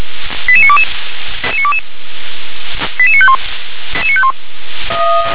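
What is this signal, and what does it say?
Five-tone selective call (ZVEI), heard as quick runs of five tones, tones of about 70 ms. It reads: 88922 88951
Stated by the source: Five-tone ZVEI selective call 88679 received on an SDR.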